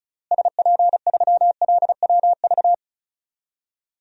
Morse code, heard as a keyed tone, about 34 wpm, 700 Hz. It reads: SP3LWV
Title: DXCC-100 Call Signs - 6 Characters - Repeated 2x - 35wpm